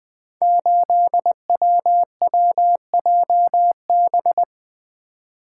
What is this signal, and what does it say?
Morse code, heard as a keyed tone, about 20 wpm, 700 Hz. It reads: 8WWJB
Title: Code Groups - 5 Letter:Numbers - 20wpm